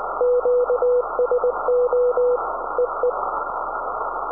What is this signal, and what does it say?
Morse code, heard as a keyed tone, about 20 wpm, 490 Hz, with noise. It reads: QSO EE
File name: iw0ffk / 2008 05 21 EI7IX IW0FFK